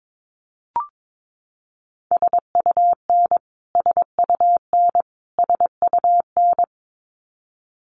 Morse code, heard as a keyed tone, about 22 wpm, 700 Hz. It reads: SUN SUN SUN
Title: Top 300 Words Repeated 22wpm